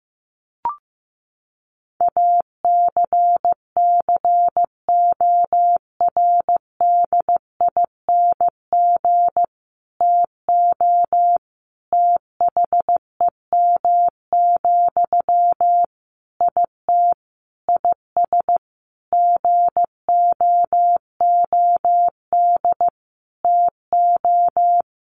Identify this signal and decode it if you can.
Morse code, heard as a keyed tone, about 15 wpm, 700 Hz. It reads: ACCORDING TO THEM, IT IS GOOD TO
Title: Sentences from Top 800 Words 15wpm